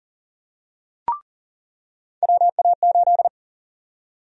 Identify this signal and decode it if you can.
Morse code, heard as a keyed tone, about 39 wpm, 700 Hz. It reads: WA8
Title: USA Call Sign Prefixes - 40wpm